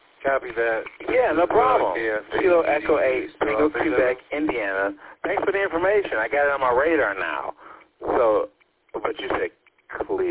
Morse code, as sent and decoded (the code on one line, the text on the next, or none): none